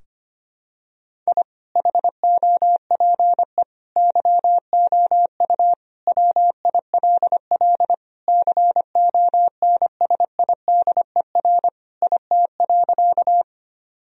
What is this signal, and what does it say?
Morse code, heard as a keyed tone, about 25 wpm, 700 Hz. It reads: I HOPE YOU WILL CONSIDER IT.
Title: Sentences from Top 1000 Words - Review Part 4 - 25wpm